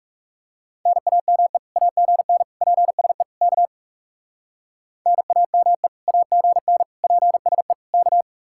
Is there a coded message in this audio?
Yes: Morse code from 0.9 to 8.2 s, heard as a keyed tone, about 38 wpm, 700 Hz. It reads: NAME AGN PSE K NAME AGN PSE K